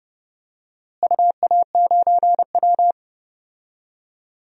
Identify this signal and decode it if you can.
Morse code, heard as a keyed tone, about 30 wpm, 700 Hz. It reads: UA9W
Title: DXCC-100 Call Signs - 4 Characters - Repeated 2x - 30wpm